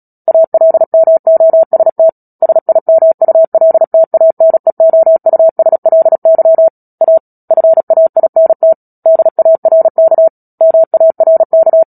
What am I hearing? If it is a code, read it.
Morse code, heard as a keyed tone, about 37 wpm, 650 Hz. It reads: ALMOST SIMULTANEOUSLY A FAINT DARK MARK
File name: Book Two / 10-Chapter 10 - Epilogue - 36 WPM